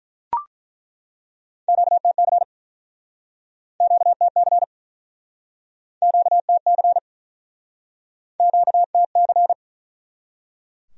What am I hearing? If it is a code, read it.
Morse code, heard as a keyed tone, about 50 wpm, 700 Hz. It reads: QTC QTC QTC QTC